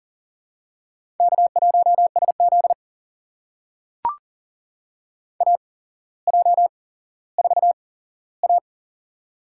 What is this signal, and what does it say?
Morse code, heard as a keyed tone, about 39 wpm, spaced as for 17 wpm, 700 Hz. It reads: K1SZ AJ4A
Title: USA Call Signs - 4 Characters - 8x Character Spacing - 40wpm